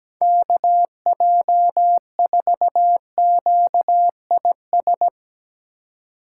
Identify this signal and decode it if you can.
Morse code, heard as a keyed tone, about 17 wpm, 700 Hz. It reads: KJ4QIS
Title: USA Call Signs - 6 Characters - 2x Character Spacing - 17wpm